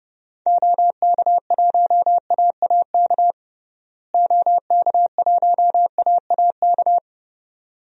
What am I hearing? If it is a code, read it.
Morse code, heard as a keyed tone, about 30 wpm, 700 Hz. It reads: OK1AAK OK1AAK
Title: DXCC-100 Call Signs - 6 Characters - Repeated 2x - 30wpm